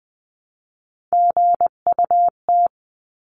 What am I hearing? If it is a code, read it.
Morse code, heard as a keyed tone, about 20 wpm, 700 Hz. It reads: GUT
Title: 259 Lesson 42d - Words 18-20wpm